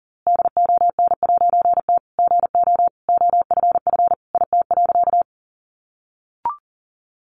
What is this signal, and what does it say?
Morse code, heard as a keyed tone, about 40 wpm, 700 Hz. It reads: DON'T GO OFF IT.
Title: A Sound Of Thunder 40wpm